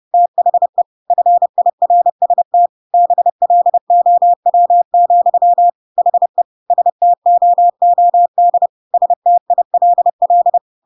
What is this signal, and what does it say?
Morse code, heard as a keyed tone, about 30 wpm, 700 Hz. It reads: THE FIRST BLOW, HE STOOD STILL
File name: Pinocchio / Pinocchio ch1 30 30 ik1hge